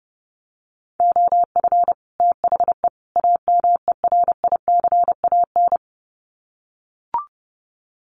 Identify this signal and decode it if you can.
Morse code, heard as a keyed tone, about 30 wpm, 700 Hz. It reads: OF THE AMERICAN
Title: Sets of 3 Words 30wpm